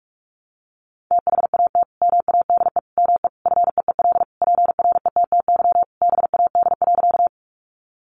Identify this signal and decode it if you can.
Morse code, heard as a keyed tone, about 44 wpm, 700 Hz. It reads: THAT MADE ME FEEL PRETTY BAD.
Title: Sentences from Top 1000 Words 45wpm